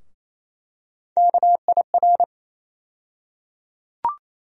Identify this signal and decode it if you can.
Morse code, heard as a keyed tone, about 28 wpm, 700 Hz. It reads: KIR
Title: USA Call Sign Postfixes - 3 Letters 28wpm